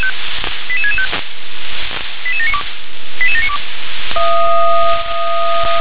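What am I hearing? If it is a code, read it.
Five-tone selective call (ZVEI), heard as quick runs of five tones, tones of about 70 ms. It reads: 88665 88792 88792
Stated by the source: ZVEI five-tone selective call received by SDR, code 88790.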